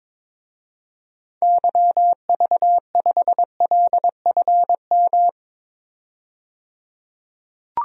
Morse code, heard as a keyed tone, about 22 wpm, 700 Hz. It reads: YV5LFM
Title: DXCC-100 Call Signs - 6 Characters - 22wpm